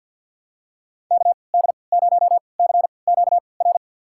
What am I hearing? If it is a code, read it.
Morse code, heard as a keyed tone, about 53 wpm, spaced as for 29 wpm, 700 Hz. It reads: KD0CQR
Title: USA Call Signs - 3x Character Spacing - 50wpm